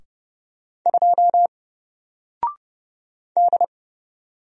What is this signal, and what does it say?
Morse code, heard as a keyed tone, about 29 wpm, 700 Hz. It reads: 2 D
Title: Single Letter-Number Alt 30wpm